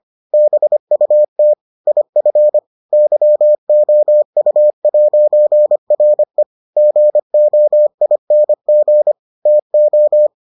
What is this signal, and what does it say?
Morse code, heard as a keyed tone, about 25 wpm, 600 Hz. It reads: BUT IF YOU'RE GOING TO